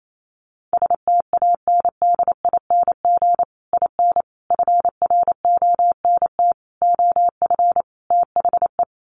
Morse code, heard as a keyed tone, about 28 wpm, 700 Hz. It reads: STANDING IN FRONT OF THE
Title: Sets of 5 Words Encore2 28wpm